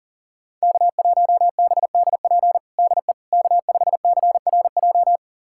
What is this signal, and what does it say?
Morse code, heard as a keyed tone, about 40 wpm, 700 Hz. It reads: K1BDP DE K5CRJ
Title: USA Callsign DE Callsign - 5 Characters - 40wpm